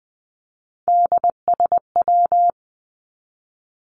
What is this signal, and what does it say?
Morse code, heard as a keyed tone, about 20 wpm, 700 Hz. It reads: DSW
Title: Abbreviations 20wpm